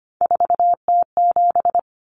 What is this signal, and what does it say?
Morse code, heard as a keyed tone, about 24 wpm, 700 Hz. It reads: 4T7
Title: Common Cut Numbers - 3-Digit 25wpm